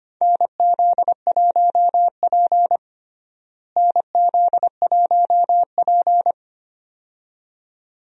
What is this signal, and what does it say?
Morse code, heard as a keyed tone, about 25 wpm, 700 Hz. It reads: NZ1P NZ1P
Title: USA Call Signs - 4 Characters - Repeated 2x - 25wpm